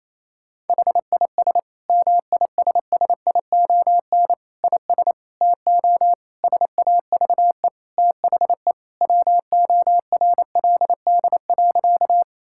Morse code, heard as a keyed tone, about 28 wpm, 700 Hz. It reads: HIS MISSION IS TO SAVE THE WORLD.
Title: Sentences from Top 1600 Words 28wpm